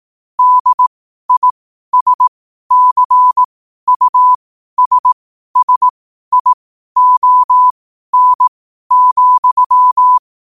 Morse code, heard as a keyed tone, about 18 wpm, spaced as for 12 wpm, 1000 Hz. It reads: DISCUSSION,
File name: qotdfiles / 12wpm